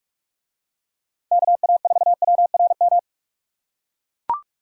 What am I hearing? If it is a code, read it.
Morse code, heard as a keyed tone, about 44 wpm, 700 Hz. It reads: KA4WRM